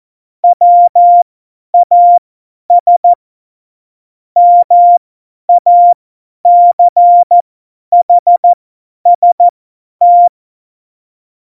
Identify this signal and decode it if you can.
Morse code, heard as a keyed tone, about 14 wpm, spaced as for 10 wpm, 700 Hz. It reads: WAS MACHST
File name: Morsekurs / Witze / Witz 31.Juli 2018, 50 F70